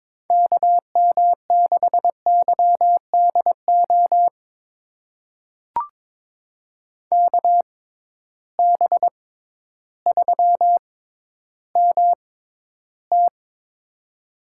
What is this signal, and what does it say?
Morse code, heard as a keyed tone, about 22 wpm, spaced as for 12 wpm, 700 Hz. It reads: KM6YDO KB3MT
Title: USA Call Signs - 6x Character Spacing - 22wpm